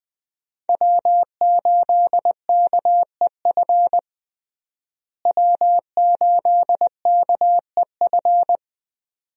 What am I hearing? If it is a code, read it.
Morse code, heard as a keyed tone, about 21 wpm, 700 Hz. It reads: W8KEF W8KEF
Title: USA Call Signs - 5 Characters - Repeated 2x - 20wpm